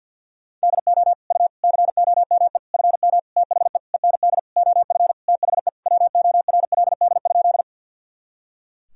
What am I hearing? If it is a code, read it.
Morse code, heard as a keyed tone, about 51 wpm, 700 Hz. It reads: DO U COME FM THE END OF THE WORLD?
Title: Sentences from Top 200 Words and Abbreviations 50wpm